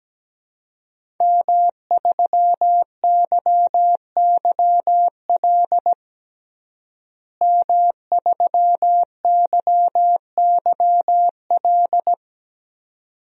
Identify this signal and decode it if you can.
Morse code, heard as a keyed tone, about 17 wpm, 700 Hz. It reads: M3YYL M3YYL
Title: DXCC-100 Call Signs - 5 Characters - Repeated 2x - 17wpm